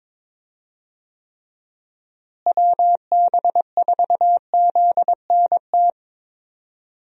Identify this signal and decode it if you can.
Morse code, heard as a keyed tone, about 22 wpm, 700 Hz. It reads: WB4ZNT
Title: USA Call Signs - 6 Characters - Repeated 2x - 22wpm